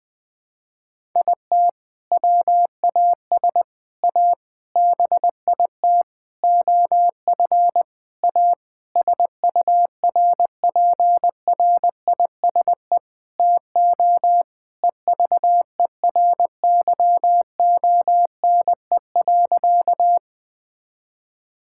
Morse code, heard as a keyed tone, about 19 wpm, 700 Hz. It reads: IT WAS A BIT OF A SURPRISE TO EVERYONE.
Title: Sentences with 6-Word Phrases - 20wpm